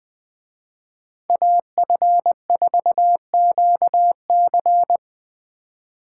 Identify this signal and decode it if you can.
Morse code, heard as a keyed tone, about 20 wpm, 700 Hz. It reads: AF4QC
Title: Speed Racing - USA 2x2 Call Signs 17wpm